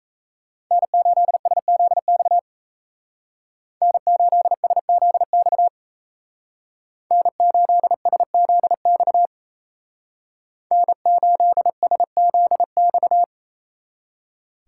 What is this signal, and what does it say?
Morse code, heard as a keyed tone, about 33 wpm, 700 Hz. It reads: N8SZX N8SZX N8SZX N8SZX
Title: Speed Racing - USA 1x3 Call Signs 28wpm